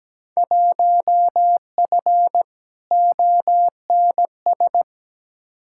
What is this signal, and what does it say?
Morse code, heard as a keyed tone, about 17 wpm, 700 Hz. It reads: 1F ONS
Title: Field Day Exchanges Encore - 17wpm